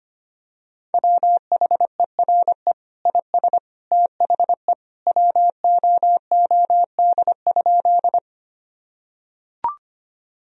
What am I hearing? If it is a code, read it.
Morse code, heard as a keyed tone, about 25 wpm, 700 Hz. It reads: WHERE IS THE WOOD?
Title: Sentences from Top 1000 Words - Review Part 2 - 25wpm